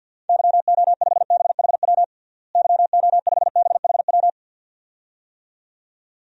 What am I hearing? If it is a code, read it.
Morse code, heard as a keyed tone, about 47 wpm, 700 Hz. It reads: YO5BHW YO5BHW
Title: DXCC-100 Call Signs - 6 Characters - Repeated 2x - 50wpm